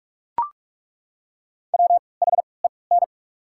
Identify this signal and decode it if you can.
Morse code, heard as a keyed tone, about 42 wpm, spaced as for 25 wpm, 700 Hz. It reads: WHEN
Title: 3x Character Spacing - Top 200 Words - 45wpm